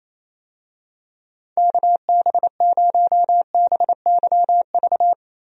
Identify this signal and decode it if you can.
Morse code, heard as a keyed tone, about 28 wpm, 700 Hz. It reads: KB0BYV
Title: USA Call Signs - 6 Characters - Repeated 3x - 28wpm